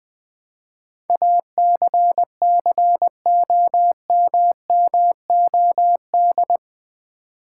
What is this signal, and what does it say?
Morse code, heard as a keyed tone, about 20 wpm, 700 Hz. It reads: ACCOMMOD
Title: Build A Word - 4 Letters At A Time - 11-Letter Words - 20wpm